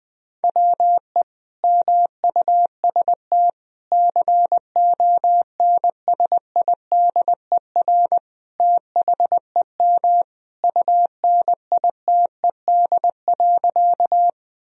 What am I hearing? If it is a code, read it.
Morse code, heard as a keyed tone, about 20 wpm, 700 Hz. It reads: WE MUST CONSIDER THEM UNITED.
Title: Sentences from Top 1000 Words - Review Part 4 - 20wpm